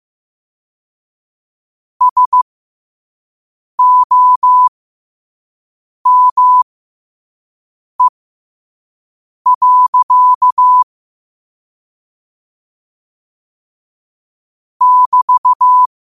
Morse code, heard as a keyed tone, about 15 wpm, spaced as for 5 wpm, 1000 Hz. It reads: SOME. <BT>
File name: qotdfiles / 5wpm